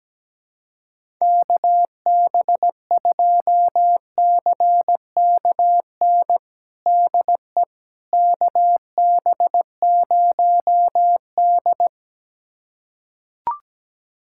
Morse code, heard as a keyed tone, about 17 wpm, 700 Hz. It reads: KB2CKN DE KB0D
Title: USA Callsign DE Callsign 17wpm